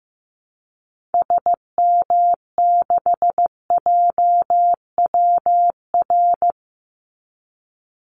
Morse code, heard as a keyed tone, about 15 wpm, 700 Hz. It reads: SM6JWR
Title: Call Signs Repeated 3x - 15wpm